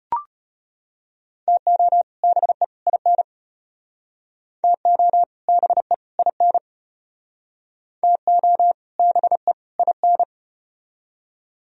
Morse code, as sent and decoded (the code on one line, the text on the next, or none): - --- / -... / .. -. / - --- / -... / .. -. / - --- / -... / .. -.
TO BE IN TO BE IN TO BE IN